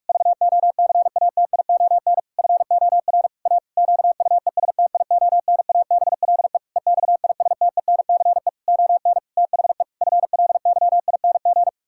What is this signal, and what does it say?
Morse code, heard as a keyed tone, about 41 wpm, 700 Hz. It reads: VOCATION FOR A QUESTIONABLE EXISTENCE ON THE FLYING
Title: Pangram Challenge v2 - 45wpm